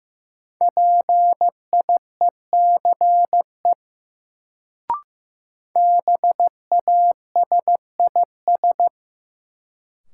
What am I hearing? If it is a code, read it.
Morse code, heard as a keyed tone, about 15 wpm, 700 Hz. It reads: PIECE BASIS
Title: Sets of 3 Words - New Word Drill - 15wpm